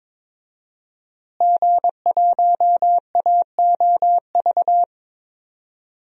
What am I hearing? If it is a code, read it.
Morse code, heard as a keyed tone, about 22 wpm, 700 Hz. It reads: G1AOV
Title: DXCC-100 Call Signs - 5 Characters - Repeated 3x - 22wpm